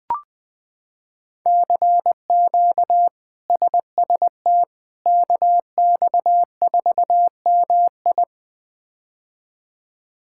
Morse code, heard as a keyed tone, about 21 wpm, 700 Hz. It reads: CQ SST KX4MI
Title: K1USN SST QSO Practice - 20wpm